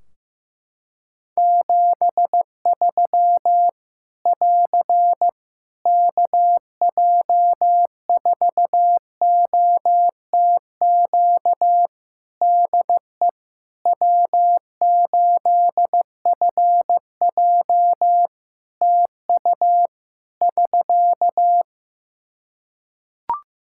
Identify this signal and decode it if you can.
Morse code, heard as a keyed tone, about 15 wpm, 700 Hz. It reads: 73 <AR> KJ4OTQ DE W8FJ TU <SK>
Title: Practice QSO - Exchange 4 - Ending - 15wpm